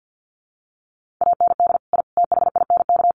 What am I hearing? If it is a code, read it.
Morse code, heard as a keyed tone, about 50 wpm, 700 Hz. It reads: AND I THINK
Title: Sets of 4 Words 50wpm